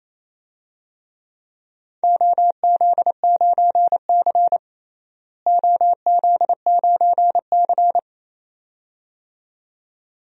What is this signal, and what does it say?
Morse code, heard as a keyed tone, about 27 wpm, 700 Hz. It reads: OZ9C OZ9C
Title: DXCC-100 Call Signs - 4 Characters - Repeated 2x - 28wpm